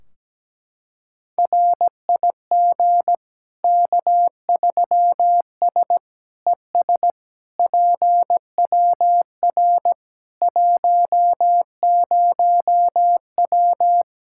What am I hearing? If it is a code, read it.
Morse code, heard as a keyed tone, about 17 wpm, 700 Hz. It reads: RIG K3S ES PWR 10W
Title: Practice QSO - Exchange 2 - Info - 17wpm